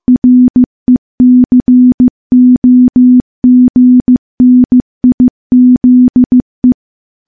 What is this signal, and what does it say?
Morse code, heard as a keyed tone, about 15 wpm, 250 Hz. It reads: RECOGNIZE
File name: morse-training-15WPM